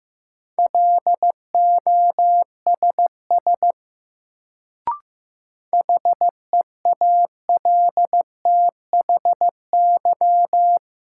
Morse code, heard as a keyed tone, about 15 wpm, 700 Hz. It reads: LOSS HEALTHY
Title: Sentences from Top 1400 Words - New Word Drill Alt - 15wpm